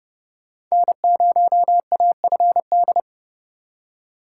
Morse code, heard as a keyed tone, about 39 wpm, 700 Hz. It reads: N0AFD